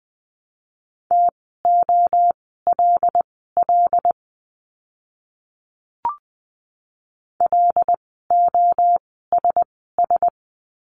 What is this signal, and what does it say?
Morse code, heard as a keyed tone, about 20 wpm, spaced as for 15 wpm, 700 Hz. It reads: TOLL LOSS
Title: 075 Lesson 13f - Letter L - Words 20wpm